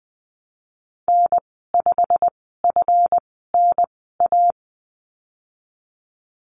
Morse code, heard as a keyed tone, about 20 wpm, spaced as for 15 wpm, 700 Hz. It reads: N5FNA
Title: 140 Lesson 23g - Letter f - Call Signs 20wpm